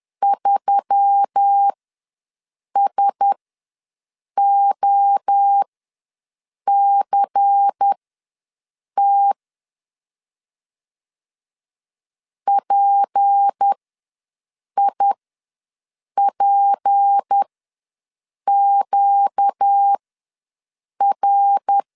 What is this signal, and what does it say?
Morse code, heard as a keyed tone, about 10 wpm, spaced as for 6 wpm, 800 Hz. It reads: 3SOCT PIPQR
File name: LEZIONI / 6 PKX0 / 6 P ws 10wpm 3751